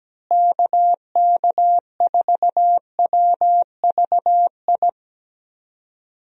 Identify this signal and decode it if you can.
Morse code, heard as a keyed tone, about 17 wpm, 700 Hz. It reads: KK4WVI
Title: USA Call Signs - 6 Characters - Repeated 3x - 17wpm